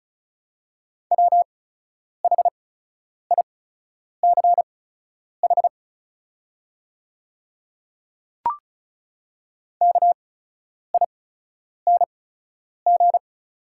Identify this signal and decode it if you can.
Morse code, heard as a keyed tone, about 36 wpm, spaced as for 10 wpm, 700 Hz. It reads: WHICH KING
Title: 8x Character Spacing - Top 500 Words - 35wpm